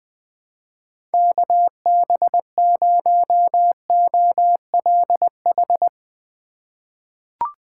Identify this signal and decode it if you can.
Morse code, heard as a keyed tone, about 20 wpm, 700 Hz. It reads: KB0OLH
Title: USA Call Signs - 6 Characters - 4x Character Spacing - 20wpm